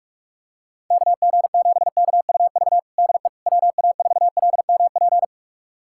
Morse code, heard as a keyed tone, about 42 wpm, 700 Hz. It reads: KG7KUV DE WA4LMP
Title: USA Callsign DE Callsign - 6 Characters - 45wpm